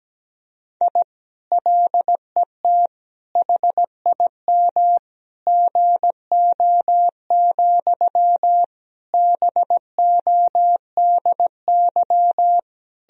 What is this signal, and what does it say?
Morse code, heard as a keyed tone, about 17 wpm, 700 Hz. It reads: I LET HIM GO, BODY
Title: Sentences from Top 1000 Words - Review Part 2 - 17wpm